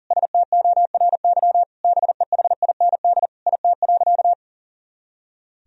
Morse code, heard as a keyed tone, about 40 wpm, 700 Hz. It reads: STORY BEHIND IT.
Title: Sentences from Top 500 Words and Abbreviations 40wpm